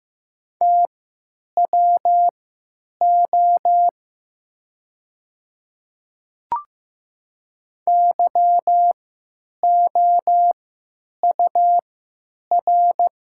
Morse code, heard as a keyed tone, about 15 wpm, spaced as for 9 wpm, 700 Hz. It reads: TWO YOUR
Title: 3x Character Spacing - Top 200 Words - 15wpm